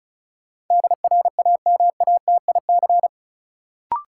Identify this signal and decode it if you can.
Morse code, heard as a keyed tone, about 36 wpm, 700 Hz. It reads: DRAMATIC
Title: Sentences from Top 1300 Words - New Word Drill - 35wpm